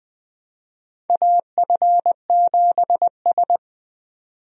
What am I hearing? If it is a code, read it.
Morse code, heard as a keyed tone, about 20 wpm, 700 Hz. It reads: AF7S
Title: USA Call Signs - 4 Characters - 20wpm